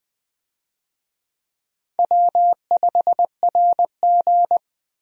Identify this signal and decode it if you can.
Morse code, heard as a keyed tone, about 20 wpm, 700 Hz. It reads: W5RG
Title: USA Call Signs - 4 Characters - 20wpm